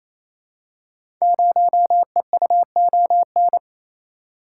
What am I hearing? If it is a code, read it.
Morse code, heard as a keyed tone, about 29 wpm, 700 Hz. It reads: 0EUON
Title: Code Groups - 5 Letter:Numbers - 28wpm